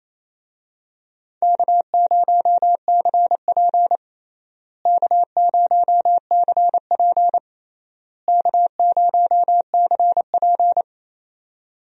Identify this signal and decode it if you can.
Morse code, heard as a keyed tone, about 28 wpm, 700 Hz. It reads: K0CP K0CP K0CP
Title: USA Call Signs - 4 Characters - Repeated 3x - 28wpm